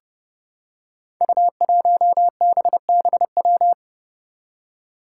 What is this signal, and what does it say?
Morse code, heard as a keyed tone, about 30 wpm, 700 Hz. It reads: U1BBW
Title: Five Challenging Letter-Number Combinations 30wpm